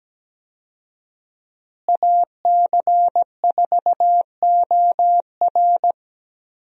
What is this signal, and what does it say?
Morse code, heard as a keyed tone, about 17 wpm, 700 Hz. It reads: AC4OR
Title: USA Call Signs - 5 Characters - Repeated 3x - 17wpm